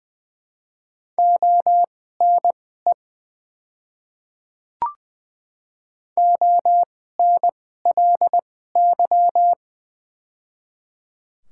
Morse code, heard as a keyed tone, about 20 wpm, spaced as for 14 wpm, 700 Hz. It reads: ONE ONLY
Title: 2x Character Spacing - Top 200 Words - 20wpm